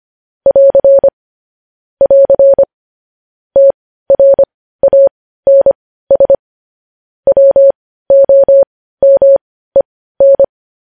Morse code, heard as a keyed tone, about 25 wpm, spaced as for 15 wpm, 550 Hz. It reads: <AR> <AR> TRANS WOMEN
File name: podcast / News headlines-1x-25x15